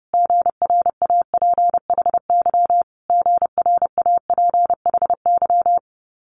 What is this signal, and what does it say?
Morse code, heard as a keyed tone, about 30 wpm, 700 Hz. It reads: GRAPHY GRAPHY
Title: Multi-letter Phonemes Repeated 3x 30wpm